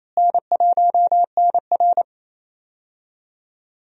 Morse code, heard as a keyed tone, about 28 wpm, 700 Hz. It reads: N1NR
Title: USA Call Signs - 4 Characters - Repeated 2x - 28wpm